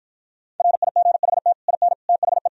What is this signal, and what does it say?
Morse code, heard as a keyed tone, about 52 wpm, 700 Hz. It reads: RIGHT IN THE